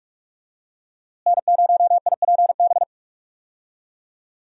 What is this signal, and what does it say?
Morse code, heard as a keyed tone, about 46 wpm, 700 Hz. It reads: N0IPB
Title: USA Call Signs - 5 Characters - Repeated 2x - 45wpm